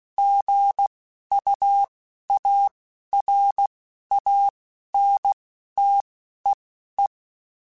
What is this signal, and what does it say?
Morse code, heard as a keyed tone, about 16 wpm, spaced as for 11 wpm, 800 Hz. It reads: GUARANTEE
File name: Chuckles / Chuckles-mp3s / Guaranteed @10